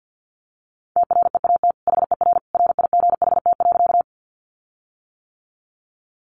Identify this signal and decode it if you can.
Morse code, heard as a keyed tone, about 50 wpm, 700 Hz. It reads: TREAT HER RIGHT.
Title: Sentences from Top 1300 Words 50wpm